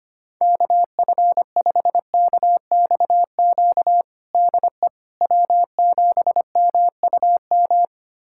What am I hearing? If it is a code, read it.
Morse code, heard as a keyed tone, about 25 wpm, 700 Hz. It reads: KF5KXQ DE W7MUM
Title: USA Callsign DE Callsign 25wpm